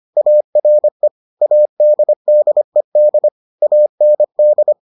Morse code, heard as a keyed tone, about 25 wpm, 600 Hz. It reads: ARE ADDED AND